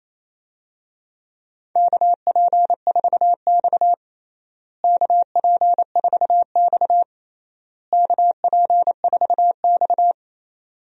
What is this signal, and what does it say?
Morse code, heard as a keyed tone, about 28 wpm, 700 Hz. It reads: KP4X KP4X KP4X
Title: DXCC-100 Call Signs - 4 Characters - Repeated 3x - 28wpm